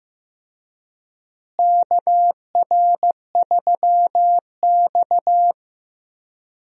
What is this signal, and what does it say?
Morse code, heard as a keyed tone, about 16 wpm, 700 Hz. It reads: KR3X
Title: USA Call Signs - 4 Characters - Repeated 2x - 15wpm